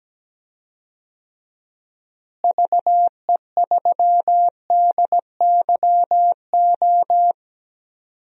Morse code, heard as a keyed tone, about 17 wpm, 700 Hz. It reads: VE3DYO
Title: DXCC-100 Call Signs - 6 Characters - Repeated 3x - 17wpm